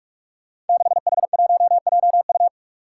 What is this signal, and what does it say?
Morse code, heard as a keyed tone, about 46 wpm, 700 Hz. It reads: 6H1JU
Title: Five Challenging Letter-Number Combinations 45wpm